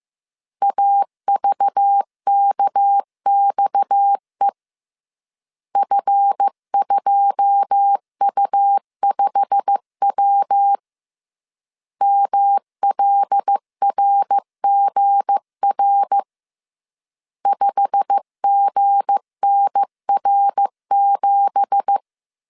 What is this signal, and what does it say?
Morse code, heard as a keyed tone, about 15 wpm, 800 Hz. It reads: AVKXE F2U5W MLRGR 5GNR7
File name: CW / DET / 1 9 1500C S A 14wpm 3222